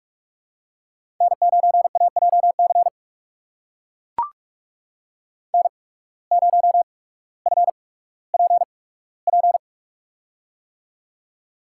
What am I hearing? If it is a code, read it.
Morse code, heard as a keyed tone, about 43 wpm, spaced as for 19 wpm, 700 Hz. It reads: N9AJC N0FPP
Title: USA Call Signs - 5 Characters - 8x Character Spacing - 45wpm